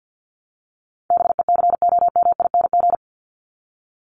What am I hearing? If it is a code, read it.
Morse code, heard as a keyed tone, about 48 wpm, 700 Hz. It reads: BECOMING
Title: ING Suffix - Top 100 Words with Base Words 50wpm